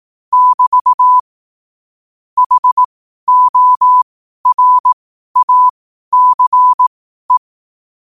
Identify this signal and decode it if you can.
Morse code, heard as a keyed tone, about 18 wpm, spaced as for 12 wpm, 1000 Hz. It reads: <BT> HORACE
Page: New quotes every day in morse code at 12 Words per minute.